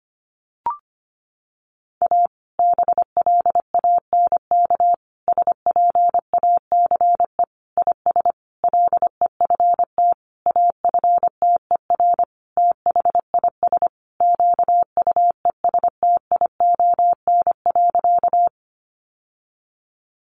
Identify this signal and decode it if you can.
Morse code, heard as a keyed tone, about 25 wpm, 700 Hz. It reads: A BLANK SPACE IS LEFT AFTER THIS QUESTION.
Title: Sentences from Top 1500 Words 25wpm